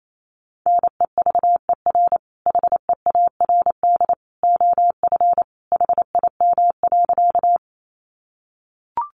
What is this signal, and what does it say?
Morse code, heard as a keyed tone, about 28 wpm, 700 Hz. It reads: NEVER HEARD OF HIM.
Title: Sentences from Top 400 Words 28wpm